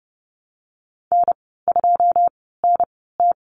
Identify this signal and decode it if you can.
Morse code, heard as a keyed tone, about 30 wpm, spaced as for 17 wpm, 700 Hz. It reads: N2NT
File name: Lesson 19g - Letter w - Call Signs 30wpm